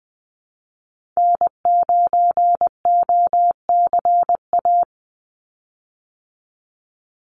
Morse code, heard as a keyed tone, about 20 wpm, 700 Hz. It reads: N9OCA
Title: USA Call Signs 20wpm